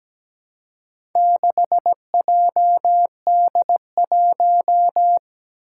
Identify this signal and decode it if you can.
Morse code, heard as a keyed tone, about 17 wpm, 700 Hz. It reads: 6JD1
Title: Four Challenging Letter-Number Combinations 17wpm